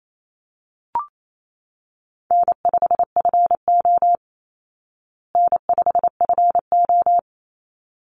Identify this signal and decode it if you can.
Morse code, heard as a keyed tone, about 28 wpm, 700 Hz. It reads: N5FO N5FO
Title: Easy Call Signs Repeated 2x - 28wpm